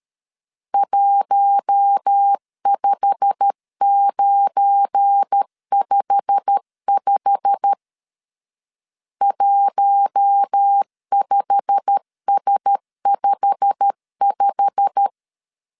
Morse code, heard as a keyed tone, about 13 wpm, 800 Hz. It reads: 15955 15S55
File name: LEZIONI / 8 YJ45 / 8 YJ45 12wpm 2151